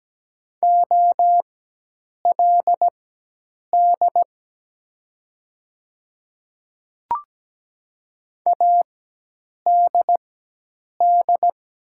Morse code, heard as a keyed tone, about 17 wpm, spaced as for 8 wpm, 700 Hz. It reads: OLD ADD
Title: 4x Character Spacing - Top 200 Words - 17wpm